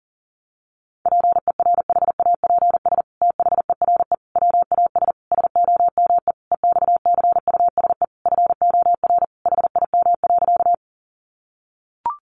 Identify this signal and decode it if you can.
Morse code, heard as a keyed tone, about 40 wpm, 700 Hz. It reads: PERHAPS THERE WAS SOME EXCUSE FOR HIM.
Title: Sentences from Top 700 Words 40wpm